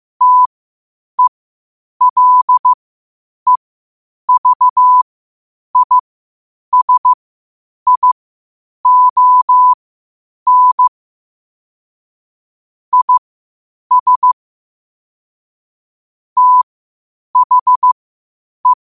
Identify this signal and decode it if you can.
Morse code, heard as a keyed tone, about 15 wpm, spaced as for 8 wpm, 1000 Hz. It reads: TELEVISION IS THE
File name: qotdfiles / 8wpm